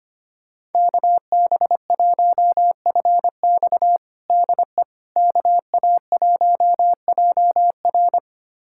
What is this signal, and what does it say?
Morse code, heard as a keyed tone, about 25 wpm, 700 Hz. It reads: KB1FX DE KA1JR
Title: USA Callsign DE Callsign - 5 Characters - 25wpm